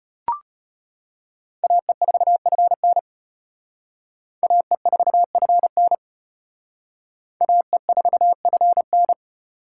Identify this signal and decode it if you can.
Morse code, heard as a keyed tone, about 32 wpm, 700 Hz. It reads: AE4FN AE4FN AE4FN